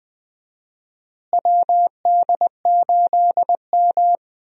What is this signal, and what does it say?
Morse code, heard as a keyed tone, about 20 wpm, 700 Hz. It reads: WD8M
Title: USA Call Signs - 4 Characters - Repeated 3x - 20wpm